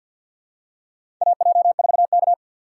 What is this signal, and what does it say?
Morse code, heard as a keyed tone, about 49 wpm, 690 Hz. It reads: AJ4K
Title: USA Call Signs - 4 Characters - 6x Character Spacing - 50wpm